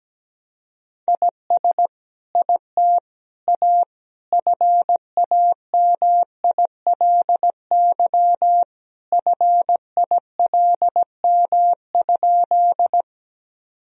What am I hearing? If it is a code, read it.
Morse code, heard as a keyed tone, about 17 wpm, 700 Hz. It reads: IS IT A FAMILY FILM?
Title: Sentences from Top 1000 Words - Review Part 3 - 17wpm